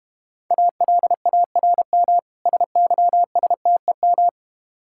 Morse code, heard as a keyed tone, about 31 wpm, 700 Hz. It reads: ALARM SYSTEM
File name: Easy Words - Speed-Racing with Hints - 25wpm